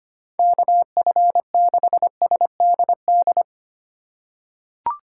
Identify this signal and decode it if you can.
Morse code, heard as a keyed tone, about 25 wpm, 700 Hz. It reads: KF6SDD
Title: USA Call Signs - 3x Character Spacing - 25wpm